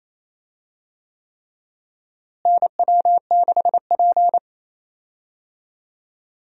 Morse code, heard as a keyed tone, about 28 wpm, 700 Hz. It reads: NW6P
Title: USA Call Signs - 4 Characters - Repeated 3x - 28wpm